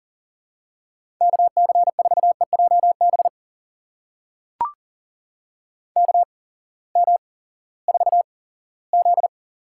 Morse code, heard as a keyed tone, about 40 wpm, spaced as for 20 wpm, 700 Hz. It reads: KC4EJB KM4Z
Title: USA Call Signs - 6 Characters - 8x Character Spacing - 40wpm